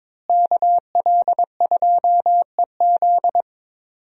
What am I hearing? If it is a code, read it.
Morse code, heard as a keyed tone, about 22 wpm, 700 Hz. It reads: KL2EZ
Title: USA Call Signs - 5 Characters - Repeated 3x - 22wpm